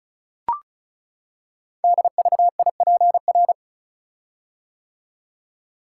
Morse code, heard as a keyed tone, about 36 wpm, 690 Hz. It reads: DVIPR
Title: Code Groups - 5 Letters - 35wpm